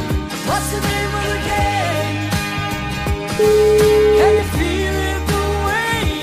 Ringback tone (Europe, 425 Hz), with noise.